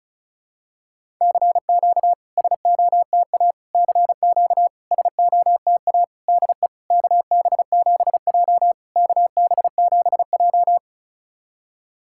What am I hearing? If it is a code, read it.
Morse code, heard as a keyed tone, about 34 wpm, 700 Hz. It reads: CQ SOTA CQ SOTA DE KB7J KB7J
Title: SOTA QSO Practice 35wpm